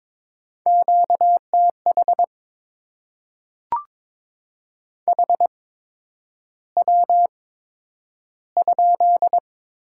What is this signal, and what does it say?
Morse code, heard as a keyed tone, about 22 wpm, spaced as for 9 wpm, 700 Hz. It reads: QTH HW?
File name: Top 20 QSO Elements - 8x Character Spacing - 22wpm